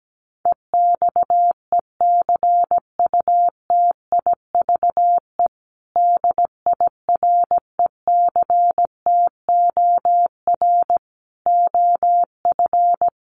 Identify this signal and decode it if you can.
Morse code, heard as a keyed tone, about 17 wpm, 700 Hz. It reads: EXECUTIVE DIRECTOR OF
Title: Sets of 5 Words Encore2 17wpm